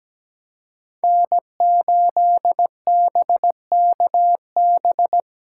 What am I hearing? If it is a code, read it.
Morse code, heard as a keyed tone, about 17 wpm, 700 Hz. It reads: N8BKB
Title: USA Call Signs - 5 Characters - 2x Character Spacing - 17wpm